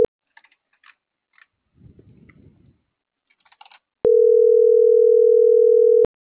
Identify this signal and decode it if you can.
Ringback tone (North America), 440 + 480 Hz.